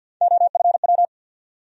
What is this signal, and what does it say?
Morse code, heard as a keyed tone, about 52 wpm, 700 Hz. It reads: OFW
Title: USA Call Sign Postfixes - 3 Letters 50wpm